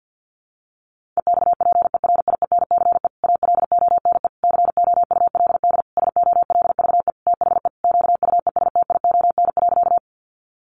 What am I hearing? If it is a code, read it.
Morse code, heard as a keyed tone, about 50 wpm, 700 Hz. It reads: EXPERIENCE ALONE COULD SOLVE THE QUESTION.
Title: Sentences from Top 1000 Words 50wpm